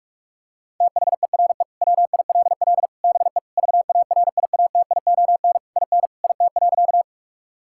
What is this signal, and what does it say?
Morse code, heard as a keyed tone, about 41 wpm, 700 Hz. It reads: THERE WILL BE VARIATION IN IT.
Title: Sentences from Top 1600 Words 45wpm